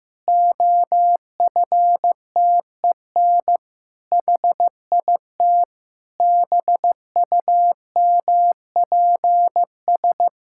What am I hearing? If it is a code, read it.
Morse code, heard as a keyed tone, about 15 wpm, 700 Hz. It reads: OFTEN HIT BUMPS
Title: Pangram Challenge v2 - 15wpm